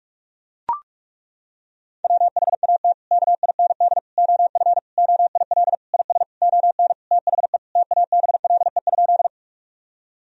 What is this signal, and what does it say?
Morse code, heard as a keyed tone, about 41 wpm, 700 Hz. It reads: WHAT KIND OF OIL IS ON THE TABLE?
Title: Sentences from Top 1000 Words - Review Part 4 - 45wpm